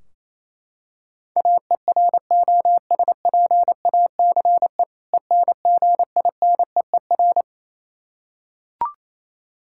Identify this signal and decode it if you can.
Morse code, heard as a keyed tone, about 28 wpm, 700 Hz. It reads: AEROSPACE ENGINEER
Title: Practice QSO - Exchange 3 - More Info - 28wpm